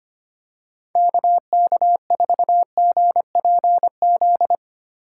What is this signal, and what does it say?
Morse code, heard as a keyed tone, about 25 wpm, 700 Hz. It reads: KK4GPZ
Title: USA Call Signs - 6 Characters - 2x Character Spacing - 25wpm